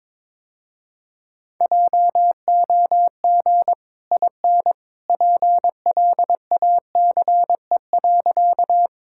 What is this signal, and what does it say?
Morse code, heard as a keyed tone, about 22 wpm, 700 Hz. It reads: JOG IN PLACE.
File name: practice / 22wpm / Easy Sentences with Hints - 22wpm